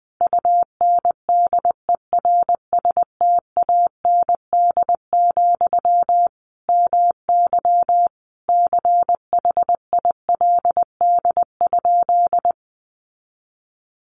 Morse code, heard as a keyed tone, about 19 wpm, 700 Hz. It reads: UNDERSTAND, MY CHILD?
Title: Sentences from Top 800 Words 20wpm